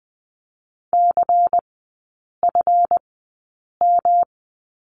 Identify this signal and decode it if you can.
Morse code, heard as a keyed tone, about 20 wpm, spaced as for 8 wpm, 700 Hz. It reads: CFM
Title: 138 Lesson 23e - Letter f - Three Characters 20wpm